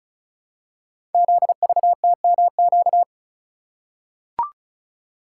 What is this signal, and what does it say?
Morse code, heard as a keyed tone, about 36 wpm, 690 Hz. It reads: ZVTMQ
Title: Code Groups - 5 Letters - 35wpm